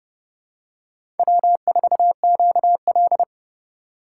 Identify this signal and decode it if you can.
Morse code, heard as a keyed tone, about 31 wpm, 700 Hz. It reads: W4QL
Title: USA Call Signs - 4 Characters - 8x Character Spacing - 30wpm